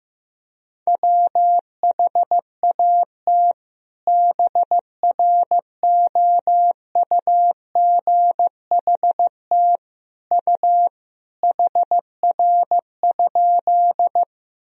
Morse code, heard as a keyed tone, about 16 wpm, 700 Hz. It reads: WHAT BROUGHT U HR?
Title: Sentences from Top 500 Words and Abbreviations 15wpm